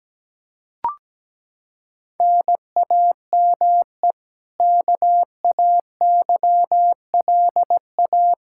Morse code, heard as a keyed tone, about 17 wpm, 700 Hz. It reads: NAME KAYLA
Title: Practice QSO - Exchange 1 - Beginning - 17wpm